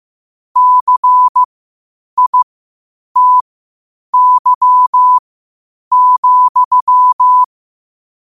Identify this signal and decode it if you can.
Morse code, heard as a keyed tone, about 15 wpm, spaced as for 8 wpm, 1000 Hz. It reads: CITY,
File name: qotdfiles / 8wpm